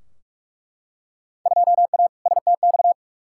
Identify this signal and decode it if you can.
Morse code, heard as a keyed tone, about 48 wpm, 700 Hz. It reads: 2A STX
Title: Field Day Exchanges - 45wpm